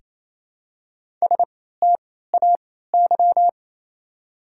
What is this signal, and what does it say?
Morse code, heard as a keyed tone, about 28 wpm, spaced as for 16 wpm, 700 Hz. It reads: STAY
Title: 3x Character Spacing - Top 500 Words - 28wpm